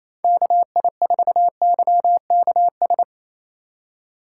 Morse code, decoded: KI4YKS

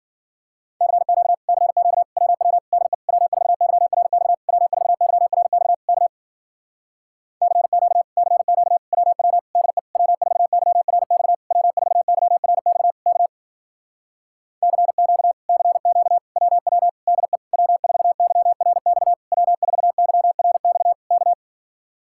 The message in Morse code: CQ CQ WW DE W4YRX W4YRX K CQ CQ WW DE W4YRX W4YRX K CQ CQ WW DE W4YRX W4YRX K

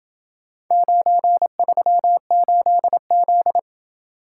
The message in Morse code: 938Z

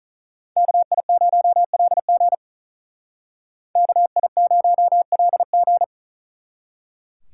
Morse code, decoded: KI0LG KI0LG

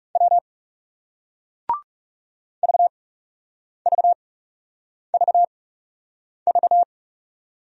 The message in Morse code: W VVVV

